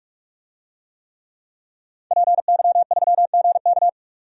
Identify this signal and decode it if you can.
Morse code, heard as a keyed tone, about 46 wpm, 700 Hz. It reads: PY3GK
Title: DXCC-100 Call Signs - 5 Characters - 45wpm